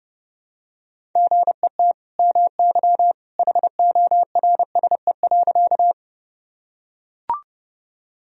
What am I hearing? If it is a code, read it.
Morse code, heard as a keyed tone, about 30 wpm, 700 Hz. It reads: GET MY HORSE.
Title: Sentences from Top 1000 Words - Review Part 1 - 30wpm